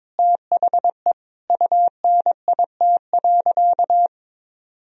Morse code, heard as a keyed tone, about 22 wpm, 700 Hz. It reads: THE UNIT.
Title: Sentences from Top 1000 Words - Review Part 2 - 22wpm